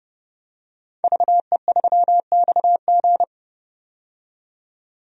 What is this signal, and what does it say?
Morse code, heard as a keyed tone, about 31 wpm, 700 Hz. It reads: VE3XG
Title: DXCC-100 Call Signs - 5 Characters - 30wpm